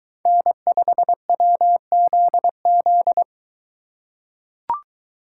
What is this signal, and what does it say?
Morse code, heard as a keyed tone, about 21 wpm, 700 Hz. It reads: N5WZZ